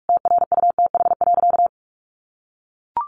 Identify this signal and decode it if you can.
Morse code, heard as a keyed tone, about 44 wpm, 700 Hz. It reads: TRUTH.
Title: Sentences from Top 1400 Words 45wpm